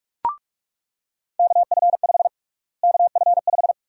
Morse code, heard as a keyed tone, about 42 wpm, 700 Hz. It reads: KF5 KF5
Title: USA Call Sign Prefixes - Repeated 2x - 45wpm